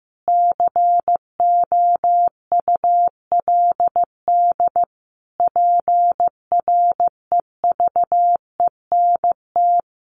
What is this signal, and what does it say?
Morse code, heard as a keyed tone, about 15 wpm, 700 Hz. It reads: COULD PREVENT
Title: Sentences from Top 1100 Words 15wpm